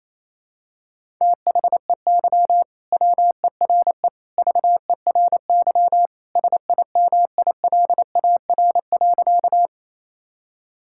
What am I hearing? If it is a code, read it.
Morse code, heard as a keyed tone, about 28 wpm, 700 Hz. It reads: THEY WERE VERY SIMILAR.
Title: Sentences from Top 1000 Words - Review Part 5 - 28wpm